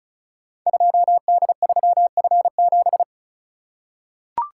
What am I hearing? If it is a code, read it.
Morse code, heard as a keyed tone, about 36 wpm, 700 Hz. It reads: 2D3F7
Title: Code Groups - 5 Letter:Numbers - 35wpm